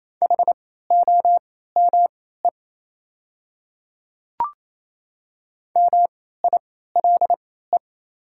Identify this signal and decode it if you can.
Morse code, heard as a keyed tone, about 27 wpm, spaced as for 16 wpm, 690 Hz. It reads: HOME MILE
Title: 3x Character Spacing - Top 300 Words - 28wpm